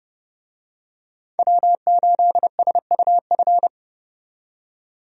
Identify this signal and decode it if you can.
Morse code, heard as a keyed tone, about 30 wpm, 700 Hz. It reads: W8SUF
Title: USA Call Signs - 5 Characters - 30wpm